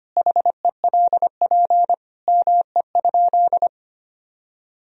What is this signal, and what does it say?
Morse code, heard as a keyed tone, about 25 wpm, 700 Hz. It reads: HELP ME?
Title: Sentences from Top 1000 Words - Review Part 3 - 25wpm